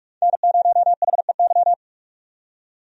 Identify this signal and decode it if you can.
Morse code, heard as a keyed tone, about 44 wpm, 700 Hz. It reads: N0HEY
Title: USA Call Signs - 5 Characters - 45wpm